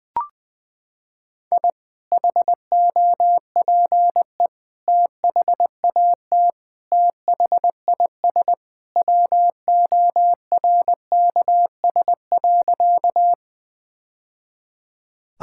Morse code, heard as a keyed tone, about 21 wpm, 700 Hz. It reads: I HOPE THAT THIS WORKS.
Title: Sentences with 2-Word Phrases - 20wpm